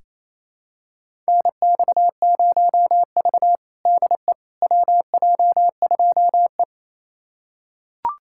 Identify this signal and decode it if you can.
Morse code, heard as a keyed tone, about 28 wpm, 700 Hz. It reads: NX0V DE WJ2E
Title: USA Callsign DE Callsign - 4 Characters - 28wpm